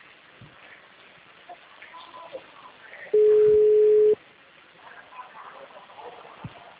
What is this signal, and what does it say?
Ringback tone (Europe, 425 Hz), with speech.